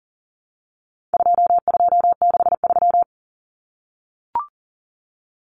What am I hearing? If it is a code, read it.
Morse code, heard as a keyed tone, about 40 wpm, 700 Hz. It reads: 2263
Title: Common Cut Numbers - 4-Digit 40wpm